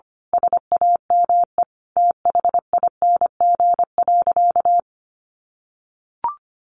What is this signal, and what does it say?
Morse code, heard as a keyed tone, about 25 wpm, 700 Hz. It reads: SAME THING.
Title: Sentences from Top 1100 Words 25wpm